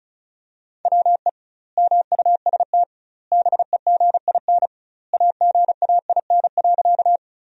Morse code, heard as a keyed tone, about 34 wpm, 700 Hz. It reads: WE MUST BEGIN AGAIN.
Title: 2x Word-Spacing - Sentences from Top 300 Words - 35wpm